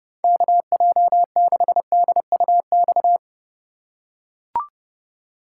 Morse code, heard as a keyed tone, about 30 wpm, 700 Hz. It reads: KJ6DUX
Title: USA Call Signs - 6 Characters - 3x Character Spacing - 30wpm